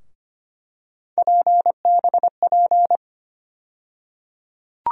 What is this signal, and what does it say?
Morse code, heard as a keyed tone, about 25 wpm, 700 Hz. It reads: PBP
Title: USA Call Sign Postfixes - 3 Letters 25wpm